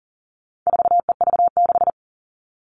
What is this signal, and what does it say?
Morse code, heard as a keyed tone, about 40 wpm, 700 Hz. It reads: 4EV6
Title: All Cut Numbers - 4-Digit 40wpm